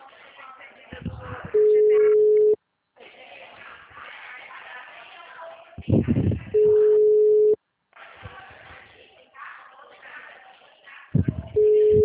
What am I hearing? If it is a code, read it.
Ringback tone (Europe, 425 Hz), with speech.
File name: out- -2049- - - .74045